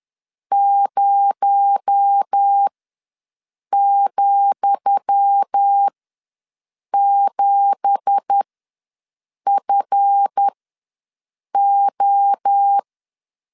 Morse code, heard as a keyed tone, about 10 wpm, spaced as for 6 wpm, 800 Hz. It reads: 0,7FO
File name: LEZIONI / 10 SIGNS / 10D COMA ws 10wpm 4314